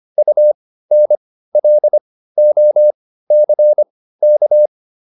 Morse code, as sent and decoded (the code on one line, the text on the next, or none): ..- -. .-.. --- -.-. -.-
UNLOCK